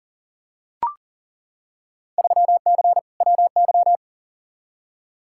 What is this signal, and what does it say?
Morse code, heard as a keyed tone, about 40 wpm, 700 Hz. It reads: 3C WY
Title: Field Day Exchanges Encore - 40wpm